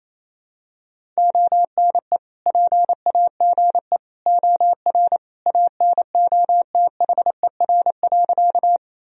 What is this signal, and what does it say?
Morse code, heard as a keyed tone, about 27 wpm, 700 Hz. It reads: ONE PAGE OR ANOTHER.
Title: Sentences from Top 1000 Words - Review Part 1 - 28wpm